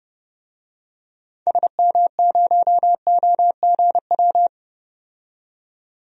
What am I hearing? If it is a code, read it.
Morse code, heard as a keyed tone, about 31 wpm, 700 Hz. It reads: SM0OGW
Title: DXCC-100 Call Signs - 6 Characters - 30wpm